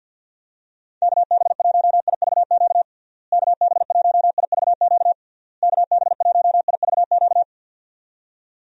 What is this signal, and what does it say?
Morse code, heard as a keyed tone, about 52 wpm, 700 Hz. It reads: KB1IVQ KB1IVQ KB1IVQ
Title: USA Call Signs - 6 Characters - Repeated 3x - 50wpm